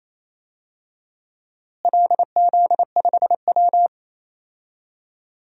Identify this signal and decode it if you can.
Morse code, heard as a keyed tone, about 28 wpm, 700 Hz. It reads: LZ5W
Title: DXCC-100 Call Signs - 4 Characters - Repeated 3x - 28wpm